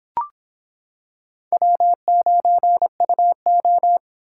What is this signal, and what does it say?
Morse code, heard as a keyed tone, about 26 wpm, 700 Hz. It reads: W9UO